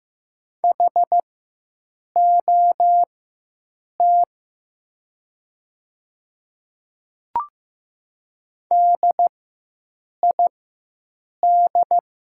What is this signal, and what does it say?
Morse code, heard as a keyed tone, about 15 wpm, spaced as for 7 wpm, 700 Hz. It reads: HOT DID
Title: 4x Character Spacing - Top 100 Words - 15wpm